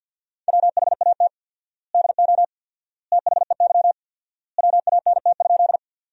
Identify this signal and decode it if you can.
Morse code, heard as a keyed tone, about 50 wpm, 700 Hz. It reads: WHAT DO THEY WANT?
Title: 4x Word-Spacing - Sentences from Top 200 Words - 50wpm